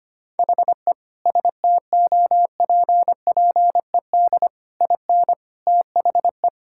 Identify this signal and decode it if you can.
Morse code, heard as a keyed tone, about 25 wpm, 700 Hz. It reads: HE STOPPED IN THE
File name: Sentences with 3-Word Phrases - 25wpm